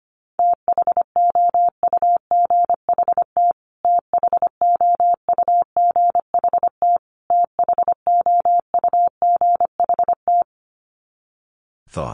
Morse code, heard as a keyed tone, about 25 wpm, 700 Hz. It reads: THOUGHT THOUGHT THOUGHT
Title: Top 500 Words Repeated 25wpm